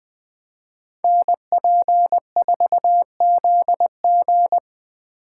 Morse code, heard as a keyed tone, about 24 wpm, 700 Hz. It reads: NP4ZG
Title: Speed Racing - USA 2x2 Call Signs 20wpm